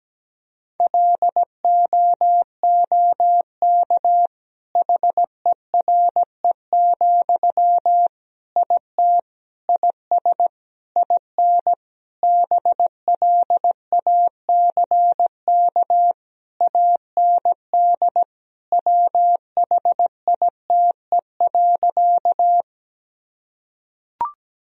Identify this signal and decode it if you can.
Morse code, heard as a keyed tone, about 17 wpm, 700 Hz. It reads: LOOK HERE, IT IS IN BLACK AND WHITE.
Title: Sentences from Top 1000 Words - Review Part 2 - 17wpm